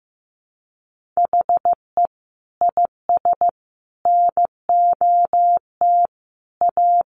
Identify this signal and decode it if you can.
Morse code, heard as a keyed tone, about 14 wpm, 700 Hz. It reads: HE IS NOT A
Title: Sentences from Top 1400 Words 15wpm